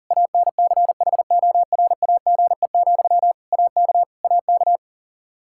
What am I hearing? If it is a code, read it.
Morse code, heard as a keyed tone, about 40 wpm, 700 Hz. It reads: ANCHORAGE, AK AK
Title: Practice QSO - Exchange 1 - Beginning - 40wpm